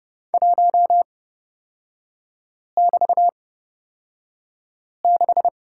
Morse code, heard as a keyed tone, about 30 wpm, spaced as for 5 wpm, 700 Hz. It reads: 1<BT>6
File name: Lesson 36e - BT - Three Characters 30wpm